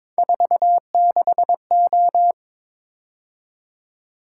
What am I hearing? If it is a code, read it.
Morse code, heard as a keyed tone, about 22 wpm, 700 Hz. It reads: 46O
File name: practice / 22wpm / Three Letter-Number Combinations - 22wpm